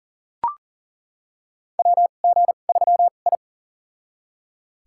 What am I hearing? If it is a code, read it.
Morse code, heard as a keyed tone, about 37 wpm, spaced as for 28 wpm, 700 Hz. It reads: WG3I
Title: USA Call Signs - 4 Characters - 2x Character Spacing - 40wpm